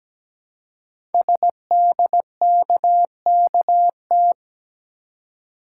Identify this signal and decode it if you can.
Morse code, heard as a keyed tone, about 17 wpm, 700 Hz. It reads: SDKKT
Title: Code Groups - 5 Letters - 17wpm